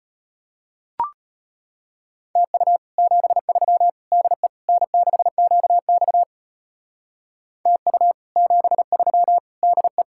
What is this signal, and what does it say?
Morse code, heard as a keyed tone, about 33 wpm, 700 Hz. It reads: TU 73 DE N6QX TU 73 DE